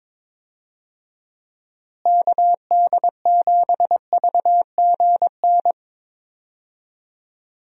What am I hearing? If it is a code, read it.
Morse code, heard as a keyed tone, about 22 wpm, 700 Hz. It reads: KD7VGN
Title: USA Call Signs - 6 Characters - Repeated 2x - 22wpm